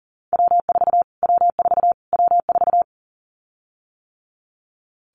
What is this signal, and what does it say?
Morse code, heard as a keyed tone, about 40 wpm, 700 Hz. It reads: W4 W4 W4
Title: Callsign Prefixes for Top 25 DXCC - Repeated 3x - 40wpm